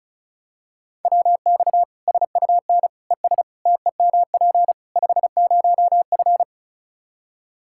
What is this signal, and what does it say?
Morse code, heard as a keyed tone, about 34 wpm, 700 Hz. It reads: WX SUN ES TEMP 50F
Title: Practice QSO - Exchange 2 - Info - 35wpm